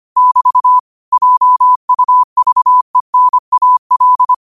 Morse code, heard as a keyed tone, about 25 wpm, 1000 Hz. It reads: <BT> JUVENAL
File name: qotdfiles / 25wpm